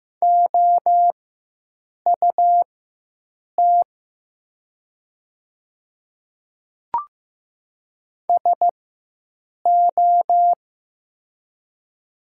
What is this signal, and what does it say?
Morse code, heard as a keyed tone, about 15 wpm, spaced as for 7 wpm, 700 Hz. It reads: OUT SO
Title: 4x Character Spacing - Top 100 Words - 15wpm